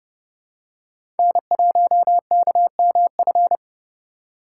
Morse code, heard as a keyed tone, about 30 wpm, 700 Hz. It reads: N1KMF
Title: USA Call Signs - 5 Characters - 3x Character Spacing - 30wpm